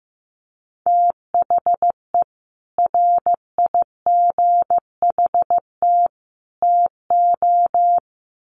Morse code, heard as a keyed tone, about 15 wpm, 700 Hz. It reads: THE RIGHT TO
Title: Sets of 3 Words 15wpm